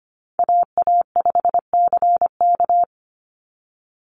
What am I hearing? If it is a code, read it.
Morse code, heard as a keyed tone, about 25 wpm, 700 Hz. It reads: AA5CK
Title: NA SOTA Chaser Callsigns 25wpm